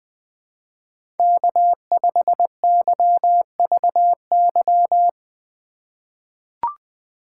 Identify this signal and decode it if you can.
Morse code, heard as a keyed tone, about 20 wpm, 700 Hz. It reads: K5YVY
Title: K1USN SST QSO Practice - 20wpm